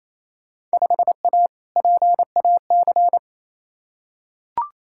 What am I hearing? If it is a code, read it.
Morse code, heard as a keyed tone, about 28 wpm, 700 Hz. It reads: 5A PAC
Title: Field Day Exchanges Encore - 28wpm